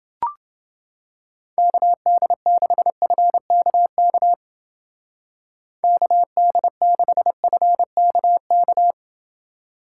Morse code, heard as a keyed tone, about 29 wpm, 700 Hz. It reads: KD6FKK KD6FKK